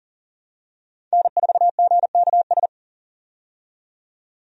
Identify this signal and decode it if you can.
Morse code, heard as a keyed tone, about 40 wpm, 700 Hz. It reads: N4GKS
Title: USA Call Signs - 5 Characters - 40wpm